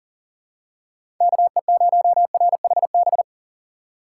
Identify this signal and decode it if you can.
Morse code, heard as a keyed tone, about 39 wpm, 700 Hz. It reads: KE0RHB
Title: USA Call Signs - 6 Characters - 4x Character Spacing - 40wpm